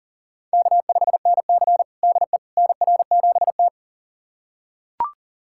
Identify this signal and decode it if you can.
Morse code, heard as a keyed tone, about 40 wpm, 700 Hz. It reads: K5NC DE NR7T
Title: USA Callsign DE Callsign - 4 Characters - 40wpm